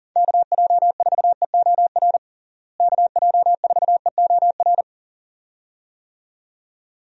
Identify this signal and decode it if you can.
Morse code, heard as a keyed tone, about 40 wpm, 700 Hz. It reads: KJ4EOR KJ4EOR
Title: USA Call Signs - 6 Characters - Repeated 2x - 40wpm